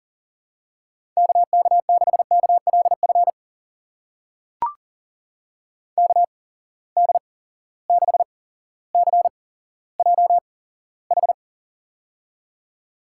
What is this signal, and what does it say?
Morse code, heard as a keyed tone, about 38 wpm, spaced as for 17 wpm, 700 Hz. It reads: KK6KLF KD6CJH
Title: USA Call Signs - 8x Character Spacing - 40wpm